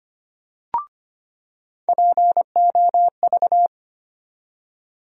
Morse code, heard as a keyed tone, about 25 wpm, 700 Hz. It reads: POV
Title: USA Call Sign Postfixes - 3 Letters 25wpm